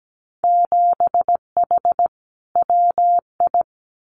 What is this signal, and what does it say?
Morse code, heard as a keyed tone, about 17 wpm, 700 Hz. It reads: 7H WI
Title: Winter Field Day Exchanges Encore 17wpm